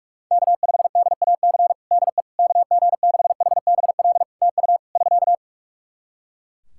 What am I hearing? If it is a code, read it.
Morse code, heard as a keyed tone, about 45 wpm, 700 Hz. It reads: K5DAC DE KG6HBL TU <SK>
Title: K1USN SST QSO Practice 45wpm